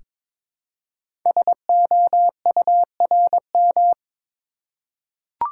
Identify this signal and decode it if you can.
Morse code, heard as a keyed tone, about 22 wpm, 700 Hz. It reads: SOURM
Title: Code Groups - 5 Letters - 22wpm